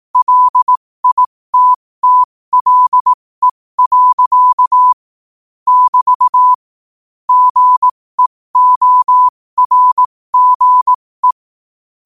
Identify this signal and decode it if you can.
Morse code, heard as a keyed tone, about 18 wpm, spaced as for 15 wpm, 1000 Hz. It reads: LITTLE. <BT> GEORGE
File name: qotdfiles / 15wpm